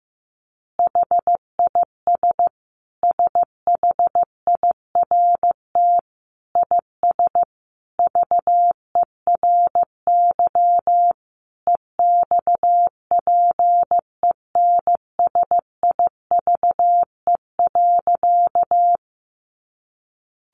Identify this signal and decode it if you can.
Morse code, heard as a keyed tone, about 15 wpm, 700 Hz. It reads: HIS SHIRT IS VERY EXPENSIVE.
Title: Additional Easy Phrases 15wpm